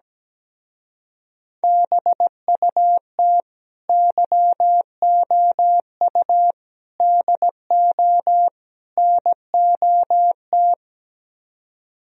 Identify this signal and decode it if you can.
Morse code, heard as a keyed tone, about 17 wpm, 700 Hz. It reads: BUT YOU DO NOT
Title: Speed-Racing - Sets of 4 Words 17wpm